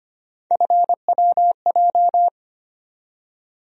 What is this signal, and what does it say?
Morse code, heard as a keyed tone, about 25 wpm, 700 Hz. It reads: FWJ
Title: USA Call Sign Postfixes - 3 Letters 25wpm